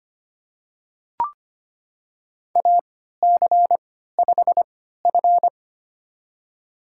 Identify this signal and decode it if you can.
Morse code, heard as a keyed tone, about 25 wpm, spaced as for 14 wpm, 700 Hz. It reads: AC5F
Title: USA Call Signs - 4 Characters - 3x Character Spacing - 25wpm